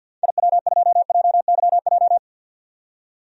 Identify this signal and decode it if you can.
Morse code, heard as a keyed tone, about 52 wpm, 700 Hz. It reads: IW2JYJ
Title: DXCC-100 Call Signs - 6 Characters - 50wpm